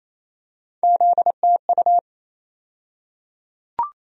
Morse code, heard as a keyed tone, about 28 wpm, 700 Hz. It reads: ZTU
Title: USA Call Sign Postfixes - 3 Letters 28wpm